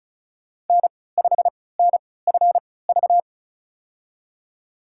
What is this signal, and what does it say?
Morse code, heard as a keyed tone, about 34 wpm, spaced as for 20 wpm, 700 Hz. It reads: N5NFV
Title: USA Call Signs - 5 Characters - 3x Character Spacing - 35wpm